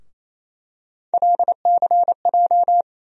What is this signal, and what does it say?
Morse code, heard as a keyed tone, about 28 wpm, 700 Hz. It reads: LCJ
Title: USA Call Sign Postfixes - 3 Letters 28wpm